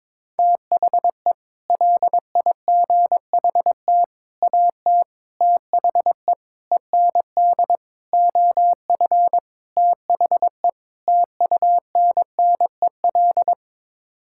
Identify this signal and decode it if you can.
Morse code, heard as a keyed tone, about 22 wpm, 700 Hz. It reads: THE LIGHT AT THE END OF THE TUNNEL
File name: practice / 22wpm / Sets of 8 Words - 22wpm